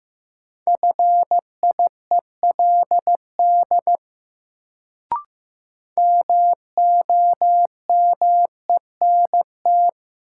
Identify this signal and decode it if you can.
Morse code, heard as a keyed tone, about 15 wpm, 700 Hz. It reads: FIELD MOMENT
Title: Sentences from Top 1000 Words - New Word Drill - 15wpm